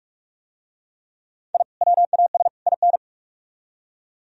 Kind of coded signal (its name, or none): Morse code